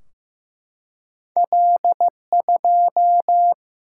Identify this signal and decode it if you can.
Morse code, heard as a keyed tone, about 15 wpm, 700 Hz. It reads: L2
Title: Single-Letter Single-Number 15wpm